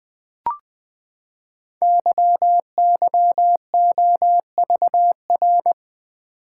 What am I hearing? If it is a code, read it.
Morse code, heard as a keyed tone, about 20 wpm, 700 Hz. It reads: YYOVR
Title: Code Groups - 5 Letters - 20wpm